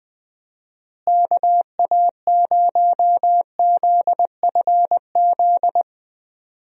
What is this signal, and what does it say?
Morse code, heard as a keyed tone, about 21 wpm, 700 Hz. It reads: KA0ZFZ
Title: USA Call Signs - 6 Characters - Repeated 2x - 20wpm